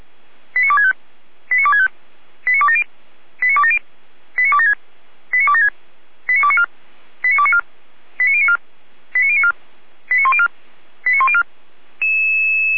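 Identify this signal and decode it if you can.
Five-tone selective call (ZVEI), heard as quick runs of five tones, tones of about 70 ms. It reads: 89265 89265 89270 89270 89276 89276 89294 89294 89094 89094 89104 89104
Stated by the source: ZVEI five-tone selective call 89250 received on an SDR.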